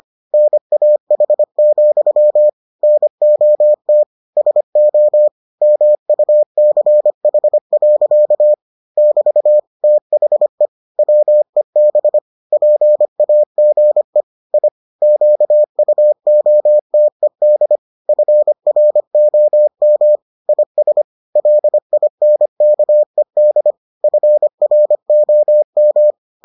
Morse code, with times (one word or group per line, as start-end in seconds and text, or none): 0.3-2.5 NAH,
2.8-4.0 NOT
4.4-5.3 SO
5.6-8.5 MUCH.
9.0-9.6 <BT>
9.8-10.7 THE
11.0-12.2 WEB
12.5-14.2 PAGE
14.5-14.7 I
15.0-17.8 QUOTED
18.1-20.2 FROM
20.5-21.0 IS
21.4-23.7 LINKED
24.0-26.1 FROM